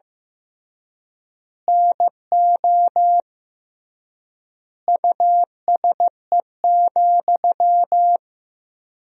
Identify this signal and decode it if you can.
Morse code, heard as a keyed tone, about 15 wpm, 700 Hz. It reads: NO USE,
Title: 3x Word-Spacing - Sentences from Top 500 Words - 15wpm